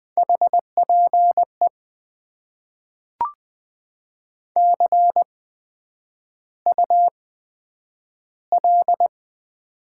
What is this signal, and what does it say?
Morse code, heard as a keyed tone, about 20 wpm, spaced as for 9 wpm, 700 Hz. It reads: HPE CUL
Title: Top 90 QSO Elements - 8x Character Spacing - 20wpm